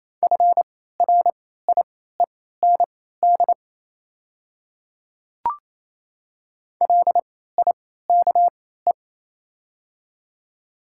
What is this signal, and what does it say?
Morse code, heard as a keyed tone, about 28 wpm, spaced as for 16 wpm, 700 Hz. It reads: FRIEND LIKE
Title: 3x Character Spacing - Top 300 Words - 28wpm